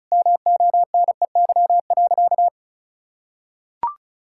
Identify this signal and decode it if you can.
Morse code, heard as a keyed tone, about 36 wpm, 700 Hz. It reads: MONEY.
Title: Sentences from Top 1000 Words - Review Part 5 - 35wpm